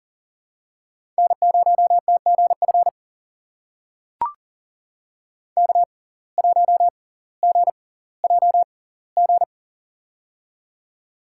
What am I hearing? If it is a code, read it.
Morse code, heard as a keyed tone, about 40 wpm, spaced as for 20 wpm, 700 Hz. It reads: N0TGF K1GJG
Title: USA Call Signs - 5 Characters - 6x Character Spacing - 40wpm